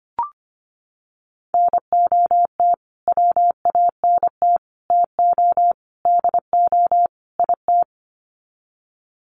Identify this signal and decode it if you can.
Morse code, heard as a keyed tone, about 25 wpm, 700 Hz. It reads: NOT WANT TO DO IT
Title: Sets of 5 Words Encore 25wpm